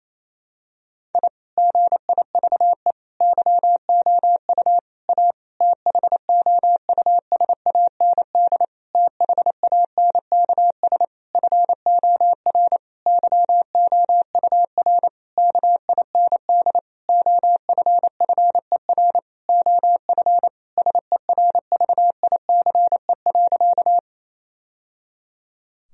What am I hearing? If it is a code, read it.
Morse code, heard as a keyed tone, about 28 wpm, 700 Hz. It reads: I GIVE YOU A THOUSAND THANKS FOR YOUR KIND OFFER OF SERVICE.
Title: Sentences from Top 1000 Words - Review Part 4 - 28wpm